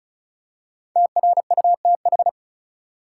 Morse code, heard as a keyed tone, about 36 wpm, 700 Hz. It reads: TRUTH
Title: Five Letter Words - 35wpm